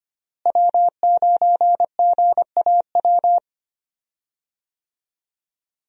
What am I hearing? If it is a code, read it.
Morse code, heard as a keyed tone, about 25 wpm, 700 Hz. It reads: W9GAW
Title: USA Call Signs - 5 Characters - Repeated 2x - 25wpm